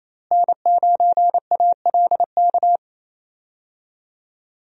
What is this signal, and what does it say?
Morse code, heard as a keyed tone, about 28 wpm, 700 Hz. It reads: N9ALK
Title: USA Call Signs - 5 Characters - Repeated 3x - 28wpm